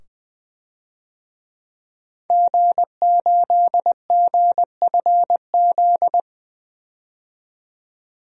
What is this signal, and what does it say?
Morse code, heard as a keyed tone, about 20 wpm, 700 Hz. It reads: G8GFZ
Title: DXCC-100 Call Signs - 5 Characters - 20wpm